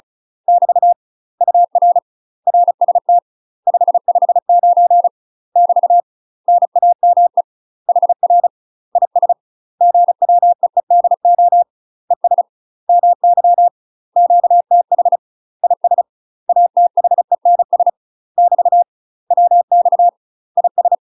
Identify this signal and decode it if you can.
Morse code, heard as a keyed tone, about 34 wpm, 700 Hz. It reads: <BT> UR RST 559 <BT> NAME HR IS GWEEDO ES MY QTH IS ATHENS <BT> WX IS
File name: UK MASTER QSOs TRAINING / 35WPM / UK MASTER QSOs 0004